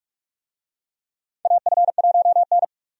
Morse code, heard as a keyed tone, about 46 wpm, 700 Hz. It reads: AF1N
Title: USA Call Signs - 4 Characters - 45wpm